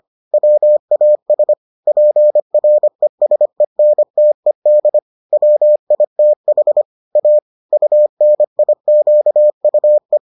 Morse code, heard as a keyed tone, about 25 wpm, 600 Hz. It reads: WAS PRESENTED WITH A UNIQUE